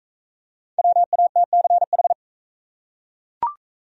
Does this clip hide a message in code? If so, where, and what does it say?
Yes: Morse code from 0.8 to 2.1 s, heard as a keyed tone, about 38 wpm, 700 Hz. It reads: WATCH